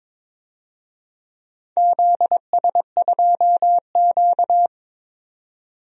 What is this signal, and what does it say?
Morse code, heard as a keyed tone, about 22 wpm, 690 Hz. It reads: ZS2Q
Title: DXCC-100 Call Signs - 4 Characters - 22wpm